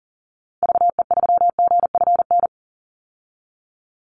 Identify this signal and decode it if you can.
Morse code, heard as a keyed tone, about 40 wpm, 700 Hz. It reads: VE3GFN
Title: Call Signs Repeated 2x - 40wpm